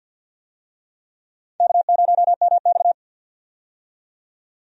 Morse code, heard as a keyed tone, about 53 wpm, 700 Hz. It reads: K0MX
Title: USA Call Signs - 4 Characters - 50wpm